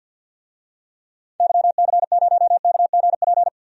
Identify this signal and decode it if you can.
Morse code, heard as a keyed tone, about 46 wpm, 700 Hz. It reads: YC0KGP
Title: DXCC-100 Call Signs - 6 Characters - 50wpm